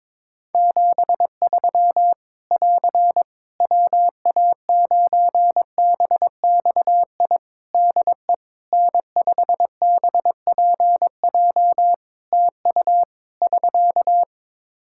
Morse code, heard as a keyed tone, about 22 wpm, 700 Hz. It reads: 73 <AR> WA9BXI DE N5BPJ TU <SK>
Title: K1USN SST QSO Practice 22wpm